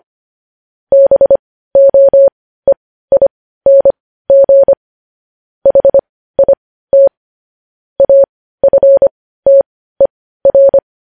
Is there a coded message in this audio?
Yes: Morse code from 0.9 to 10.8 s, heard as a keyed tone, about 25 wpm, spaced as for 15 wpm, 550 Hz. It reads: BOEING HIT AFTER